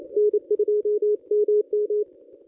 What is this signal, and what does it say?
Morse code, heard as a keyed tone, about 26 wpm, 430 Hz, with radio noise. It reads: N2MM
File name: contests / ss-cw-k5tr-op-n5rz -1100